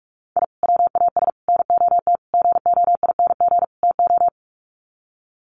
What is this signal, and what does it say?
Morse code, heard as a keyed tone, about 44 wpm, 700 Hz. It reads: I WAS NOT GOING TO
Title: Sets of 5 Words 45wpm